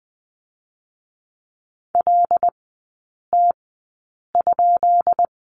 Morse code, heard as a keyed tone, about 20 wpm, 700 Hz. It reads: L T ?
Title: 198 Lesson 32e - Letter k - Three Characters 20wpm